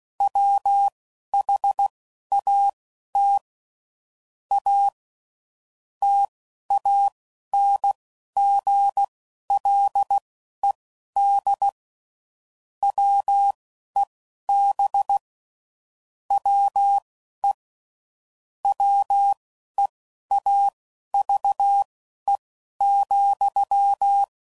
Morse code, heard as a keyed tone, about 16 wpm, spaced as for 12 wpm, 800 Hz. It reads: WHAT A TANGLED WEB WE WEAVE,